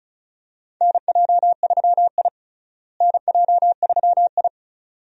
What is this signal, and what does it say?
Morse code, heard as a keyed tone, about 36 wpm, 700 Hz. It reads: NJ3I NJ3I
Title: USA Call Signs - 4 Characters - Repeated 2x - 35wpm